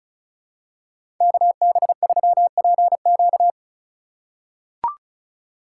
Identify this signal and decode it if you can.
Morse code, heard as a keyed tone, about 36 wpm, 700 Hz. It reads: KB3PQ
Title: USA Call Signs - 5 Characters - 4x Character Spacing - 35wpm